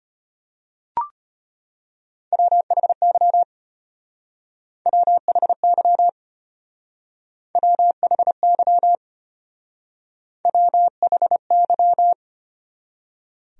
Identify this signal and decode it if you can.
Morse code, heard as a keyed tone, about 31 wpm, 700 Hz. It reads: WHY WHY WHY WHY
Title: Speed-Racing Top 500 Words 25wpm